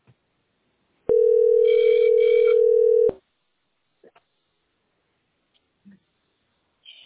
Ringback tone (North America), 440 + 480 Hz.